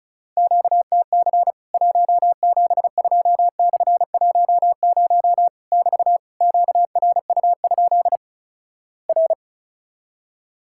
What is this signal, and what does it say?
Morse code, heard as a keyed tone, about 36 wpm, 700 Hz. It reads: QTC 172/10 <BT> QRU? R
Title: Worked All Europe DX Contest - QTC Exchange - 35wpm